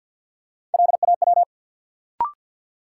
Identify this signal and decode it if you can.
Morse code, heard as a keyed tone, about 51 wpm, 700 Hz. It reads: LAW
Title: Sentences from Top 600 Words - New Word Drill - 50wpm